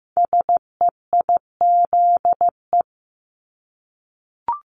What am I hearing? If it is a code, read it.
Morse code, heard as a keyed tone, about 15 wpm, 700 Hz. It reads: SEIZE
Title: Multi-letter Phonemes Repeated 3x 15wpm